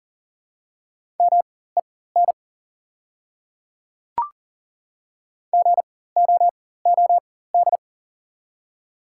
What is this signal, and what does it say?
Morse code, heard as a keyed tone, about 40 wpm, spaced as for 19 wpm, 700 Hz. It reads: MEN GOOD
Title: 4x Character Spacing - Top 200 Words - 40wpm